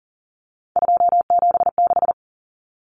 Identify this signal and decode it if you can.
Morse code, heard as a keyed tone, about 40 wpm, 700 Hz. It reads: 276
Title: Common Cut Numbers - 3-Digit 40wpm